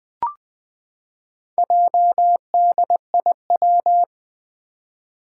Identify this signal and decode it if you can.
Morse code, heard as a keyed tone, about 20 wpm, 700 Hz. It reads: JDIW
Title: Four Challenging Letter-Number Combinations 20wpm